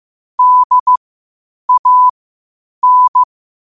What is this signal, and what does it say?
Morse code, heard as a keyed tone, about 15 wpm, spaced as for 8 wpm, 1000 Hz. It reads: DAN
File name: qotdfiles / 8wpm